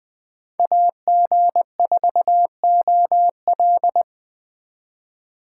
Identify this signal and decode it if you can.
Morse code, heard as a keyed tone, about 24 wpm, 700 Hz. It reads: AG4OL
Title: Speed Racing - USA 2x2 Call Signs 20wpm